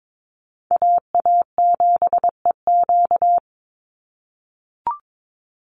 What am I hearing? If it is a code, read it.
Morse code, heard as a keyed tone, about 22 wpm, 700 Hz. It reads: AA7EQ
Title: NA SOTA Chaser Callsigns 22wpm